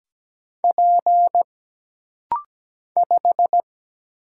Morse code, heard as a keyed tone, about 17 wpm, 700 Hz. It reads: P 5
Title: Single Letter-Number - Rapid-Fire 17wpm